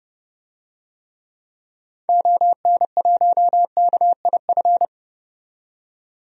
Morse code, heard as a keyed tone, about 31 wpm, 700 Hz. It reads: ON1KIF
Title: DXCC-100 Call Signs - 6 Characters - 30wpm